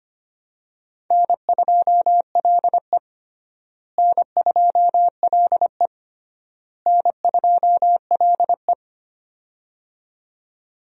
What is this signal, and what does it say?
Morse code, heard as a keyed tone, about 25 wpm, 700 Hz. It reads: N2LE N2LE N2LE
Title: USA Call Signs - 4 Characters - Repeated 3x - 25wpm